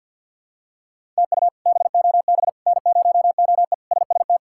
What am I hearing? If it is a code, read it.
Morse code, heard as a keyed tone, about 51 wpm, 700 Hz. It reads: TU BOB N0OE SST
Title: K1USN SST QSO Practice - 50wpm